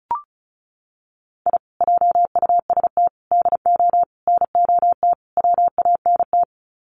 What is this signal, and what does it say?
Morse code, heard as a keyed tone, about 34 wpm, 700 Hz. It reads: I JUST DO NOT WANT
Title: Sets of 5 Words Encore2 35wpm